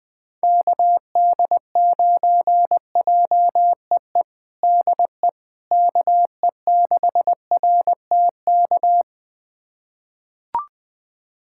Morse code, heard as a keyed tone, about 20 wpm, 700 Hz. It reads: KD9JEE DE KE6RTK
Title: USA Callsign DE Callsign - 6 Characters - 20wpm